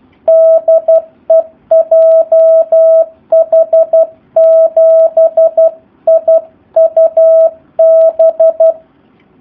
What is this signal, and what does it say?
Morse code, heard as a keyed tone, about 12 wpm, 650 Hz. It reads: DE JH7IUB
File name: jh8cbh / 2019cw / SP04- w